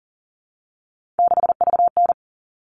Morse code, heard as a keyed tone, about 40 wpm, 700 Hz. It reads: 6VN
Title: All Cut Numbers - 3-Digit 40wpm